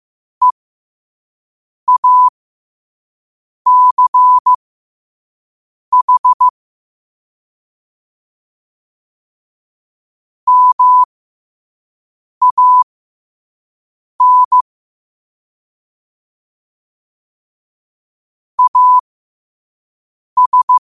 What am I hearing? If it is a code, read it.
Morse code, heard as a keyed tone, about 15 wpm, spaced as for 5 wpm, 1000 Hz. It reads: EACH MAN AS
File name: qotdfiles / 5wpm